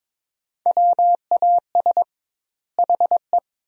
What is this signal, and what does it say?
Morse code, heard as a keyed tone, about 22 wpm, 700 Hz. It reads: WAS HE
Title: 2x Word-Spacing - Sentences from Top 400 Words - 22wpm